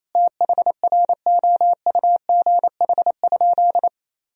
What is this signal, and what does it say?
Morse code, heard as a keyed tone, about 28 wpm, 700 Hz. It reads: THROUGH?
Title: Sentences from Top 1000 Words - Review Part 2 - 28wpm